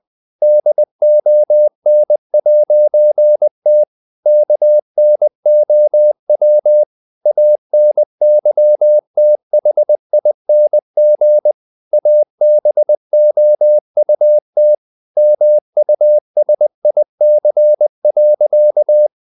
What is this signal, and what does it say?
Morse code, heard as a keyed tone, about 20 wpm, 600 Hz. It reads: DON'T KNOW ANYTHING ABOUT MUSIC.
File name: CW / Practice / EnglishQuotations0320